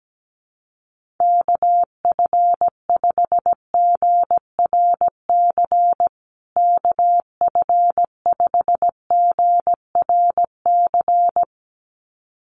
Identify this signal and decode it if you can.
Morse code, heard as a keyed tone, about 17 wpm, 700 Hz. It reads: KF5GRC KF5GRC
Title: USA Call Signs Repeated 2x 17wpm